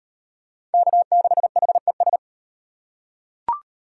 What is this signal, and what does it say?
Morse code, heard as a keyed tone, about 38 wpm, 700 Hz. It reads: K6HES